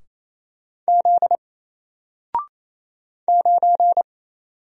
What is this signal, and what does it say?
Morse code, heard as a keyed tone, about 27 wpm, 700 Hz. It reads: Z 9
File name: Single-Letter-Number-28wpm